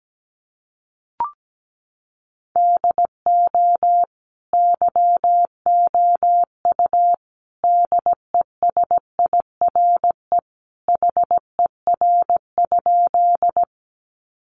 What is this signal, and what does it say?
Morse code, heard as a keyed tone, about 17 wpm, 700 Hz. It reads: DO YOU DESIRE HER?
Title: Sentences from Top 1100 Words 17wpm